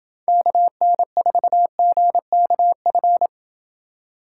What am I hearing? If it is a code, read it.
Morse code, heard as a keyed tone, about 28 wpm, 700 Hz. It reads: KN4GKF